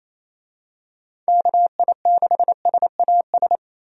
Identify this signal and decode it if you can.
Morse code, heard as a keyed tone, about 28 wpm, 700 Hz. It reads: KI6SAS
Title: USA Call Signs - 6 Characters - 28wpm